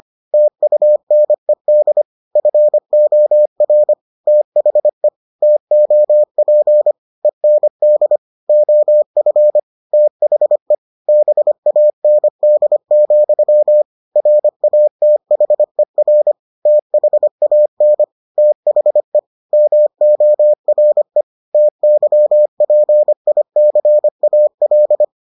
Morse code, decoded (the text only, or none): TUNED FOR THE TOP END OF THE BAND, RATHER THAN THE MORE TYPICAL